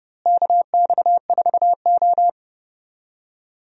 Morse code, heard as a keyed tone, about 30 wpm, 700 Hz. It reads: KX4O
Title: USA Call Signs - 4 Characters - Repeated 2x - 30wpm